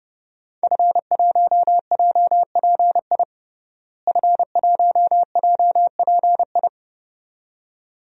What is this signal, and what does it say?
Morse code, heard as a keyed tone, about 30 wpm, 700 Hz. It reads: F1JPI F1JPI
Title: DXCC-100 Call Signs - 5 Characters - Repeated 2x - 30wpm